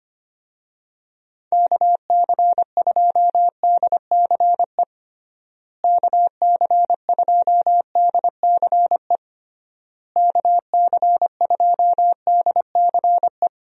Morse code, heard as a keyed tone, about 25 wpm, 700 Hz. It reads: KC2DCE KC2DCE KC2DCE
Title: USA Call Signs - 6 Characters - Repeated 3x - 25wpm